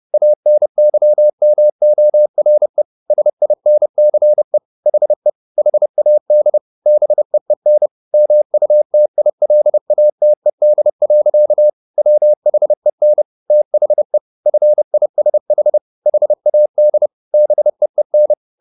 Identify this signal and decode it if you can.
Morse code, heard as a keyed tone, about 30 wpm, 600 Hz. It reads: ANYMORE SINCE HE HAD BEEN MUTILATED. WHEN THE FISH HAD BEEN